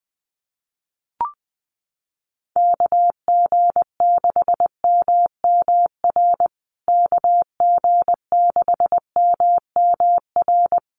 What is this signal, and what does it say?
Morse code, heard as a keyed tone, about 20 wpm, 700 Hz. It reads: KG6MMR KG6MMR
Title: USA Call Signs Repeated 3x 20wpm